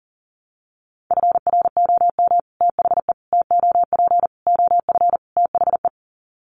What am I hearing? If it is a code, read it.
Morse code, heard as a keyed tone, about 40 wpm, 700 Hz. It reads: FROM THE TOP OF THE
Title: Sets of 5 Words Encore 40wpm